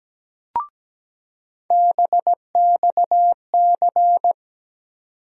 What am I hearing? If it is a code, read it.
Morse code, heard as a keyed tone, about 17 wpm, 700 Hz. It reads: BXC
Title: USA Call Sign Postfixes - 3 Letters 17wpm